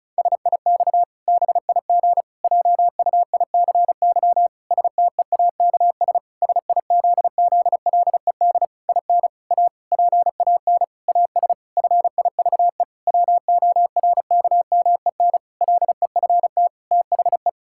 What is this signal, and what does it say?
Morse code, heard as a keyed tone, about 36 wpm, 700 Hz. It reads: SIX BIG JUICY STEAKS SIZZLED IN A PAN AS FIVE WORKMEN LEFT THE
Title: Pangram Challenge v2 - 35wpm